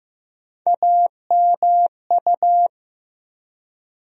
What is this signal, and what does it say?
Morse code, heard as a keyed tone, about 15 wpm, 700 Hz. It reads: AMU
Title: USA Call Sign Postfixes - 3 Letters 15wpm